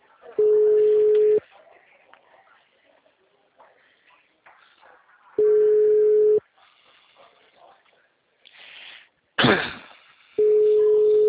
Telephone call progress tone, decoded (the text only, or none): ringback tone